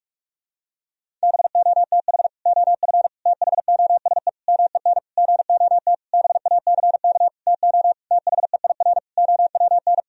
Morse code, heard as a keyed tone, about 46 wpm, 700 Hz. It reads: BOTH OF THOSE MEN GOT BACK TO THEIR OWN
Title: Sentences from Top 1000 Words - Review Part 1 - 45wpm